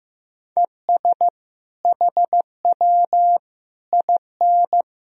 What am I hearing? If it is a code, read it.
Morse code, heard as a keyed tone, about 15 wpm, 700 Hz. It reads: ES HW IN
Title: Sentences from Top 200 Words and Abbreviations 15wpm